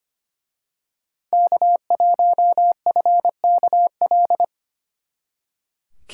Morse code, heard as a keyed tone, about 29 wpm, 700 Hz. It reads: K1FKL
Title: Speed Racing - USA 1x3 Call Signs 25wpm